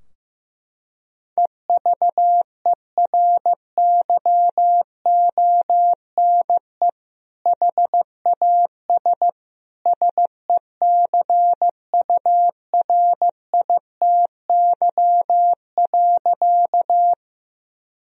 Morse code, heard as a keyed tone, about 15 wpm, 700 Hz. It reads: EVERYONE HAS SECURITY.
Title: Sentences from Top 1000 Words - Review Part 4 - 15wpm